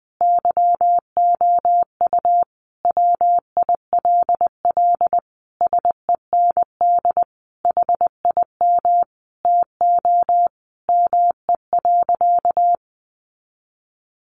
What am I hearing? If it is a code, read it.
Morse code, heard as a keyed tone, about 20 wpm, 700 Hz. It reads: YOU WILL SEND HIM TO ME.
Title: Sentences from Top 1500 Words 20wpm